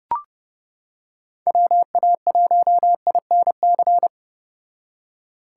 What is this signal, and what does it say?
Morse code, heard as a keyed tone, about 30 wpm, 700 Hz. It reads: WA1INC